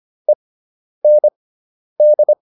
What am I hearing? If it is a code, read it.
Morse code, heard as a keyed tone, about 25 wpm, spaced as for 10 wpm, 600 Hz. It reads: END